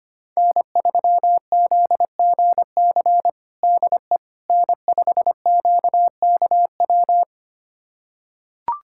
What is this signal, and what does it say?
Morse code, heard as a keyed tone, about 25 wpm, 700 Hz. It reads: N3ZGC DE N5QKW
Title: USA Callsign DE Callsign - 5 Characters - 25wpm